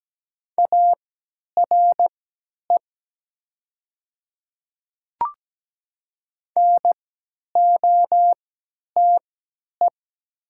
Morse code, heard as a keyed tone, about 17 wpm, spaced as for 10 wpm, 700 Hz. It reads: ARE NOTE
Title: 3x Character Spacing - Top 500 Words - 17wpm